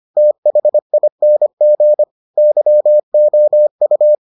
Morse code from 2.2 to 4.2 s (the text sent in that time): YOU